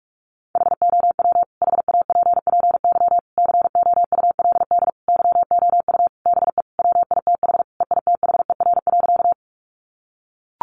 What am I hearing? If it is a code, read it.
Morse code, heard as a keyed tone, about 44 wpm, 700 Hz. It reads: HOW HAPPY COULD YOU BE WITH EITHER.
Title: Sentences from Top 1000 Words 45wpm